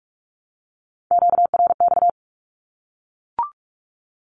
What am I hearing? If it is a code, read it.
Morse code, heard as a keyed tone, about 44 wpm, 700 Hz. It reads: QRX
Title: Q-Codes Repeated 3x 45wpm